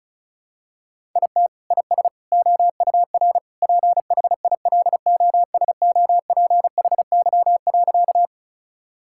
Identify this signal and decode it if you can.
Morse code, heard as a keyed tone, about 34 wpm, 700 Hz. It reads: IT IS OUR PHILOSOPHY.
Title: Sentences from Top 1000 Words - Review Part 6 - 35wpm